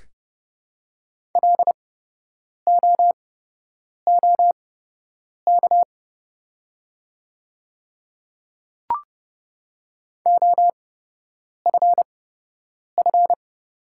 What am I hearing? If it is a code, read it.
Morse code, heard as a keyed tone, about 29 wpm, spaced as for 8 wpm, 700 Hz. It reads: LOOK OFF
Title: 8x Character Spacing - Top 200 Words - 30wpm